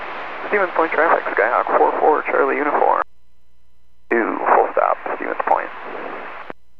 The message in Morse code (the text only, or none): none